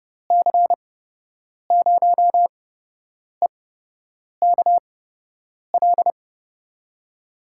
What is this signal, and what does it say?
Morse code, heard as a keyed tone, about 30 wpm, spaced as for 8 wpm, 700 Hz. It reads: C0EKL
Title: USA Call Signs - 8x Character Spacing - 30wpm